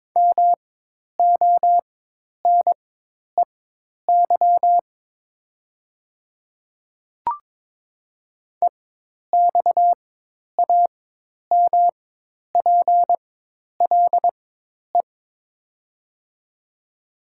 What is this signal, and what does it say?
Morse code, heard as a keyed tone, about 22 wpm, spaced as for 10 wpm, 700 Hz. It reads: MONEY EXAMPLE
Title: 4x Character Spacing - Top 500 Words - 22wpm